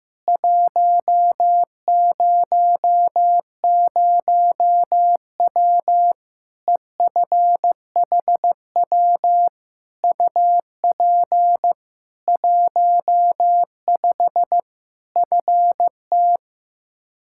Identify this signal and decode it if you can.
Morse code, heard as a keyed tone, about 15 wpm, 700 Hz. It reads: 100W EFHW UP 15 FT
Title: Practice QSO - Exchange 2 - Info - 15wpm